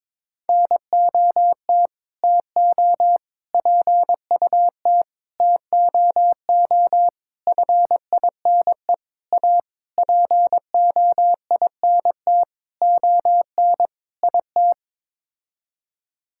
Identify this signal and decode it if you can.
Morse code, heard as a keyed tone, about 22 wpm, 700 Hz. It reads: NOT TO PUT TOO FINE A POINT ON IT
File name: Sets of 9 Words - 22wpm